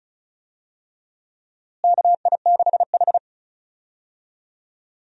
Morse code, heard as a keyed tone, about 34 wpm, 700 Hz. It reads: KI6H
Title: USA Call Signs - 4 Characters - Repeated 3x - 35wpm